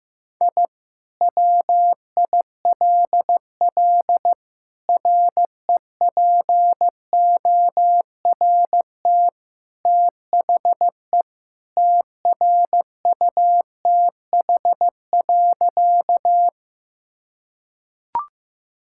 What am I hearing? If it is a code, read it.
Morse code, heard as a keyed tone, about 15 wpm, 700 Hz. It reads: I WILL REPORT THE TRUTH.
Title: Sentences from Top 1000 Words - Review Part 6 - 15wpm